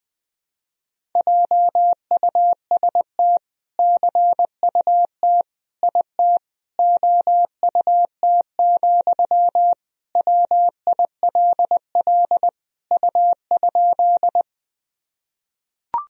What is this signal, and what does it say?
Morse code, heard as a keyed tone, about 20 wpm, 700 Hz. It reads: JUST CUT IT OUT, WILL U?
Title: Sentences from Top 300 Words and Abbreviations 20wpm